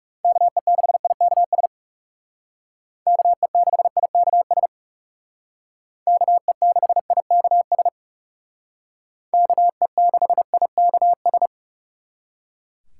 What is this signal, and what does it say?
Morse code, heard as a keyed tone, about 36 wpm, 700 Hz. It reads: KE6IKS KE6IKS KE6IKS KE6IKS
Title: Speed Racing - USA 2x3 Call Signs 30wpm